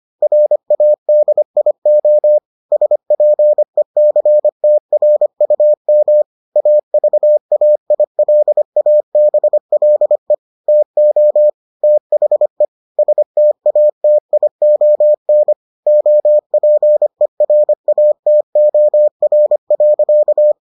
Morse code, heard as a keyed tone, about 25 wpm, 600 Hz. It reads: RADIO SPECTRUM AVAILABLE TO THE STATION OPERATOR.